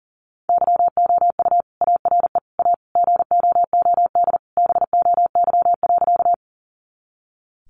Morse code, heard as a keyed tone, about 40 wpm, 700 Hz. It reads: YOU ARE A GOOD BOY.
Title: Sentences from Top 200 Words 40wpm